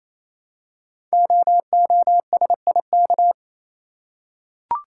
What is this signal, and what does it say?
Morse code, heard as a keyed tone, about 27 wpm, 700 Hz. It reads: OOSIK
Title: Code Groups - 5 Letters - 28wpm